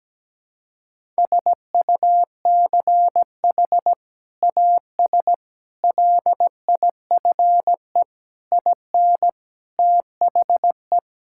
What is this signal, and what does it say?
Morse code, heard as a keyed tone, about 17 wpm, 700 Hz. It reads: SUCH AS LIFE IN THE
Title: Sentences from Top 1000 Words - Review Part 5 - 17wpm